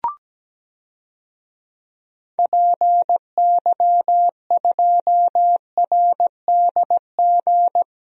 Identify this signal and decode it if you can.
Morse code, heard as a keyed tone, about 17 wpm, 700 Hz. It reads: PY2RDG
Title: DXCC-100 Call Signs - 6 Characters - Repeated 3x - 17wpm